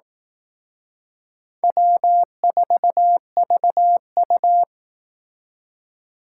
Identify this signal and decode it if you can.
Morse code, heard as a keyed tone, about 18 wpm, 700 Hz. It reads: W4VU
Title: Speed Racing - USA 1x2 Call Signs 15wpm